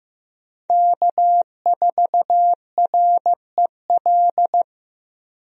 Morse code, heard as a keyed tone, about 15 wpm, 700 Hz. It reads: K4REL
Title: USA Call Signs - 5 Characters - Repeated 3x - 15wpm